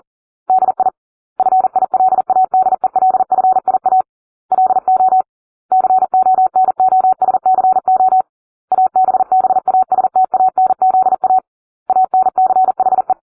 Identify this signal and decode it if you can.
Morse code, heard as a keyed tone, about 40 wpm, 750 Hz. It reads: DI FILADELFIA LO CONOSCO ABBASTANZA ANCHE